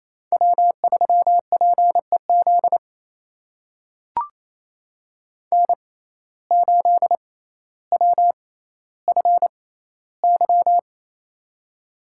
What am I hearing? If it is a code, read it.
Morse code, heard as a keyed tone, about 28 wpm, spaced as for 14 wpm, 700 Hz. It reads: W3PEZ N8WFY
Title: USA Call Signs - 5 Characters - 6x Character Spacing - 28wpm